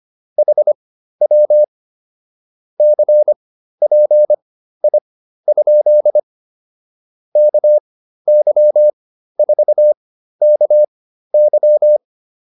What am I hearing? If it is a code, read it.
Morse code, heard as a keyed tone, about 25 wpm, spaced as for 13 wpm, 600 Hz. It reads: HW CPI? KY4KY
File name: QSO 108 13